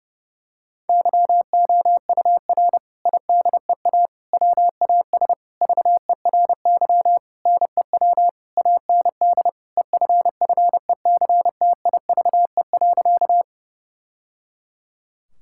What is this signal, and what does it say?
Morse code, heard as a keyed tone, about 30 wpm, 700 Hz. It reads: YOUR IDEA WAS VERY NEW AND EFFECTIVE.
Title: Sentences from Top 1000 Words - Review Part 6 - 30wpm